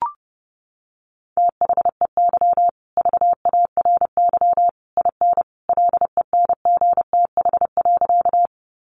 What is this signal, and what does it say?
Morse code, heard as a keyed tone, about 30 wpm, 700 Hz. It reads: THEY VARY IN LENGTH.
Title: Sentences from Top 1200 Words 30wpm